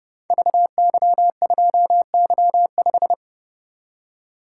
Morse code, heard as a keyed tone, about 30 wpm, 700 Hz. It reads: VY2Y5
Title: Code Groups - 5 Letter:Numbers - 30wpm